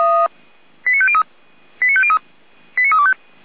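Five-tone selective call (ZVEI), heard as quick runs of five tones, tones of about 70 ms. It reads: 89503 89503 89325
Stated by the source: ZVEI five-tone selective call 89001, received on an SDR, from the feuerwehr-orlamuende.org fire brigade audio archive.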